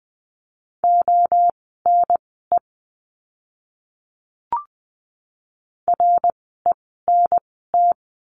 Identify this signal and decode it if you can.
Morse code, heard as a keyed tone, about 20 wpm, spaced as for 14 wpm, 700 Hz. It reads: ONE RENT
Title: 054 Lesson 10f - Letter r - Words 20wpm